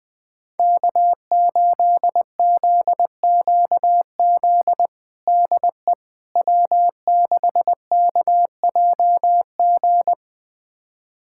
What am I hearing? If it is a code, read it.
Morse code, heard as a keyed tone, about 20 wpm, 700 Hz. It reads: K8ZQZ DE W6KJG
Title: USA Callsign DE Callsign - 5 Characters - 20wpm